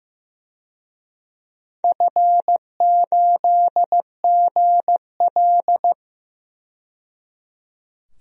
Morse code, heard as a keyed tone, about 15 wpm, 700 Hz. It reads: F8GL
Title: DXCC-100 Call Signs - 4 Characters - 15wpm